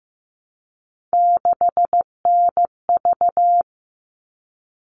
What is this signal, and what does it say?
Morse code, heard as a keyed tone, about 15 wpm, 700 Hz. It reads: 6NV
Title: All Cut Numbers - 3-Digit 15wpm